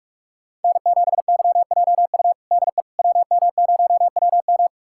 Morse code, heard as a keyed tone, about 46 wpm, 700 Hz. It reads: N7YJU DE WM0WM
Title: USA Callsign DE Callsign - 5 Characters - 45wpm